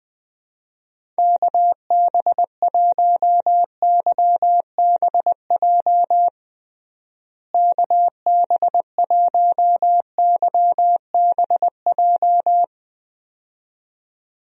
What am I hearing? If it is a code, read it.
Morse code, heard as a keyed tone, about 21 wpm, 700 Hz. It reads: KB1YBJ KB1YBJ
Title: USA Call Signs - 6 Characters - Repeated 2x - 20wpm